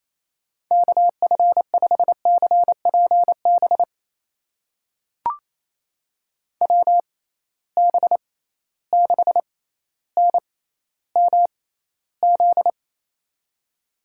Morse code, heard as a keyed tone, about 28 wpm, spaced as for 14 wpm, 700 Hz. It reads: KF5CPB WB6NMZ
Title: USA Call Signs - 6 Characters - 6x Character Spacing - 28wpm